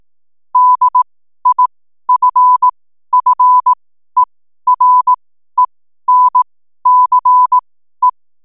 Morse code, heard as a keyed tone, about 18 wpm, spaced as for 12 wpm, 1000 Hz. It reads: DIFFERENCE
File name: n7net / 12wpm